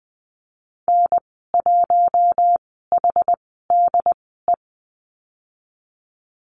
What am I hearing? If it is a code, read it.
Morse code, heard as a keyed tone, about 20 wpm, spaced as for 14 wpm, 700 Hz. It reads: N1HDE
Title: 076 Lesson 13g - Letter L - Call Signs 20wpm